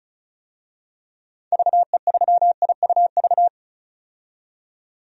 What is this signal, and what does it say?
Morse code, heard as a keyed tone, about 36 wpm, 700 Hz. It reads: VE3IUV
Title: DXCC-100 Call Signs - 6 Characters - 35wpm